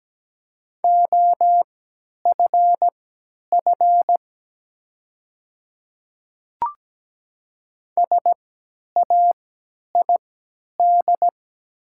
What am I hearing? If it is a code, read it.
Morse code, heard as a keyed tone, about 17 wpm, spaced as for 10 wpm, 700 Hz. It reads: OFF SAID
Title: 3x Character Spacing - Top 300 Words - 17wpm